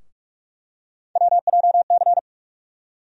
Morse code, heard as a keyed tone, about 43 wpm, 700 Hz. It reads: WJC
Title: USA Call Sign Postfixes - 3 Letters 45wpm